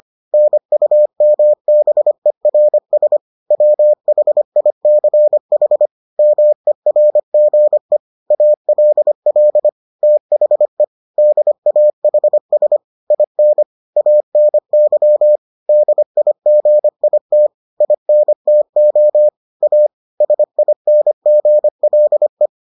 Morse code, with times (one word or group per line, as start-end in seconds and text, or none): 0.3-3.2 NUMBERS
3.5-5.9 WHICH
6.2-8.0 MERGE
8.3-9.7 ALL
10.0-10.8 THE
11.2-12.8 DAHS
13.1-13.6 IN
14.0-15.4 ANY
15.7-17.5 DIGIT
17.8-19.3 INTO
19.6-19.9 A
20.2-22.5 SINGLE